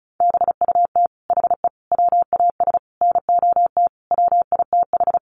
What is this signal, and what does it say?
Morse code, heard as a keyed tone, about 34 wpm, 700 Hz. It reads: BUT HE WAS NOT WITH
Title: Sentences from Top 1500 Words 35wpm